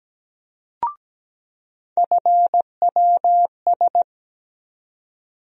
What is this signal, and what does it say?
Morse code, heard as a keyed tone, about 17 wpm, 700 Hz. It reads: FWS
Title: USA Call Sign Postfixes - 3 Letters 17wpm